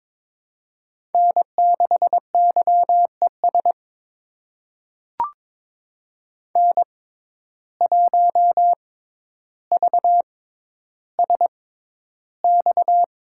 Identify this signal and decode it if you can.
Morse code, heard as a keyed tone, about 22 wpm, spaced as for 11 wpm, 700 Hz. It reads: N6YES N1VSX
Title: USA Call Signs - 5 Characters - 6x Character Spacing - 22wpm